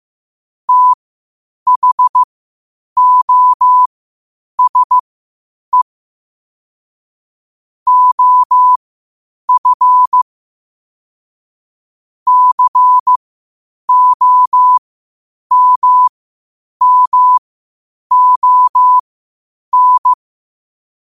Morse code, heard as a keyed tone, about 15 wpm, spaced as for 8 wpm, 1000 Hz. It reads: THOSE OF COMMON
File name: qotdfiles / 8wpm